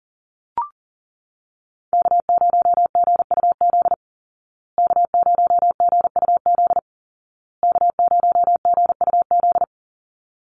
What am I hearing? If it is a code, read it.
Morse code, heard as a keyed tone, about 40 wpm, 700 Hz. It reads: K0GUZ K0GUZ K0GUZ
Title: Call Signs Repeated 2x - 40wpm